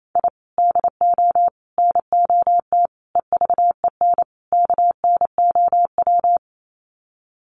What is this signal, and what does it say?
Morse code, heard as a keyed tone, about 28 wpm, 700 Hz. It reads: I DO NOT EVEN KNOW
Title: Sets of 5 Words 28wpm